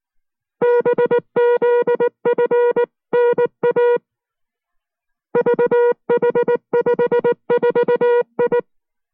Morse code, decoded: BZFNA VH54I